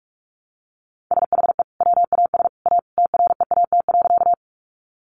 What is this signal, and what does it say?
Morse code, heard as a keyed tone, about 44 wpm, 700 Hz. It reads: SHE WAS A TREAT.
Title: Sentences from Top 1300 Words 45wpm